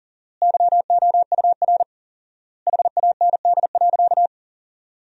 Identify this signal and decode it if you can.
Morse code, heard as a keyed tone, about 37 wpm, 700 Hz. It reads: YOUR HAND.
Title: 4x Word-Spacing - Sentences from Top 400 Words - 40wpm